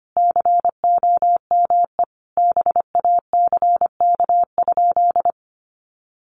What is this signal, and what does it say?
Morse code, heard as a keyed tone, about 25 wpm, 700 Hz. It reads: COME BACK?
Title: Sentences from Top 200 Words 25wpm